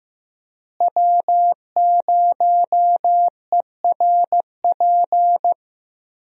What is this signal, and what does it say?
Morse code, heard as a keyed tone, about 15 wpm, 700 Hz. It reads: W0ERP
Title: USA Call Signs - 5 Characters - Repeated 2x - 15wpm